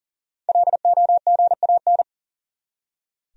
Morse code, heard as a keyed tone, about 37 wpm, 700 Hz. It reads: LOGAN
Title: USA Common Names - 40wpm